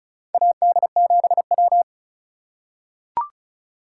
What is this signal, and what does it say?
Morse code, heard as a keyed tone, about 34 wpm, 700 Hz. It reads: AD7W
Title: USA Call Signs - 2x Character Spacing - 35wpm